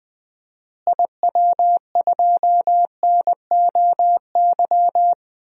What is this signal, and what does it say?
Morse code, heard as a keyed tone, about 21 wpm, 700 Hz. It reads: IW2NOY
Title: DXCC-100 Call Signs - 6 Characters - Repeated 2x - 20wpm